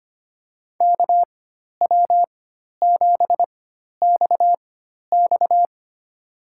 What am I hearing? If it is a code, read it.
Morse code, heard as a keyed tone, about 25 wpm, spaced as for 12 wpm, 700 Hz. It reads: KW7XX
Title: USA Call Signs - 5 Characters - 4x Character Spacing - 25wpm